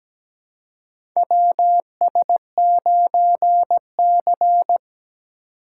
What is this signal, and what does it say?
Morse code, heard as a keyed tone, about 17 wpm, 700 Hz. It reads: WS9C
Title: USA Call Signs - 4 Characters - 2x Character Spacing - 17wpm